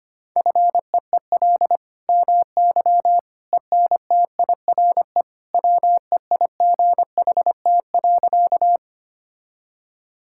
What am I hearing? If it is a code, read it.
Morse code, heard as a keyed tone, about 25 wpm, 700 Hz. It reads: FEEL MY ENTIRE WEIGHT.
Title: Sentences from Top 1000 Words - Review Part 5 - 25wpm